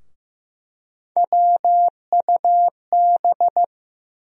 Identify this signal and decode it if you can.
Morse code, heard as a keyed tone, about 15 wpm, 700 Hz. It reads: WUB
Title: USA Call Sign Postfixes - 3 Letters 15wpm